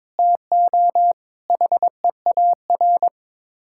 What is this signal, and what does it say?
Morse code, heard as a keyed tone, about 22 wpm, 700 Hz. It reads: TO HEAR
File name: practice / 22wpm / Build A Phrase - 5-Word Phrases - 22wpm